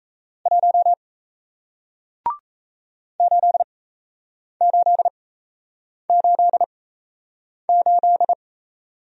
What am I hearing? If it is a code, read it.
Morse code, heard as a keyed tone, about 38 wpm, spaced as for 8 wpm, 700 Hz. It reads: 1 8888